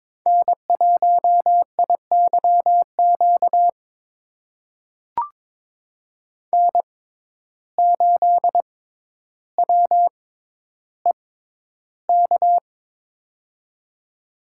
Morse code, heard as a keyed tone, about 22 wpm, spaced as for 11 wpm, 700 Hz. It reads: N1IYQ N8WEK
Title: USA Call Signs - 5 Characters - 6x Character Spacing - 22wpm